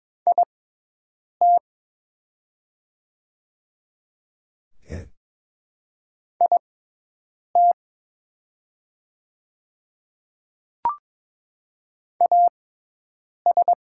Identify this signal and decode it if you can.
Morse code, heard as a keyed tone, about 22 wpm, spaced as for 8 wpm, 700 Hz. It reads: IT IT AS
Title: 6x Character Spacing - Top 500 Words - 22wpm